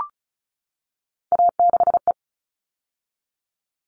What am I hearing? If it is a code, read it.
Morse code, heard as a keyed tone, about 34 wpm, 700 Hz. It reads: A6E
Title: All Cut Numbers - 3-Digit 35wpm